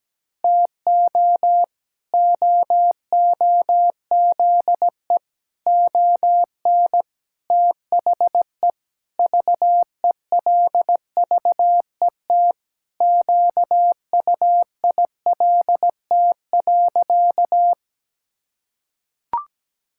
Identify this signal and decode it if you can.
Morse code, heard as a keyed tone, about 17 wpm, 700 Hz. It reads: TO OOZE ON THE VELVET QUILT.
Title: Pangram Challenge v2 - 17wpm